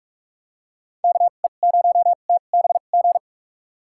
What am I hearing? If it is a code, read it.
Morse code, heard as a keyed tone, about 43 wpm, spaced as for 32 wpm, 700 Hz. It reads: KE0TBG
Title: USA Call Signs - 2x Character Spacing - 45wpm